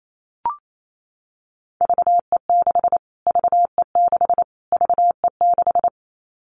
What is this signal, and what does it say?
Morse code, heard as a keyed tone, about 28 wpm, 700 Hz. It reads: VE6 VE6 VE6
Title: Callsign Prefixes for Top 25 DXCC - Repeated 3x - 28wpm